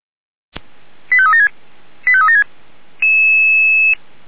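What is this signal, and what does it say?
Five-tone selective call (ZVEI), heard as quick runs of five tones, tones of about 80 ms. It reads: 85376 85376
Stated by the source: ZVEI five-tone call received on SDR.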